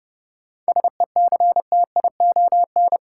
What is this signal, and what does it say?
Morse code, heard as a keyed tone, about 30 wpm, 700 Hz. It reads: SECTION
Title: Sentences from Top 1000 Words - Review Part 5 - New Word Drill - 30wpm